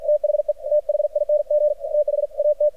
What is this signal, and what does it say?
Morse code, heard as a keyed tone, about 53 wpm, 600 Hz, with radio noise. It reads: THE THING THAT
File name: archived / CRSnet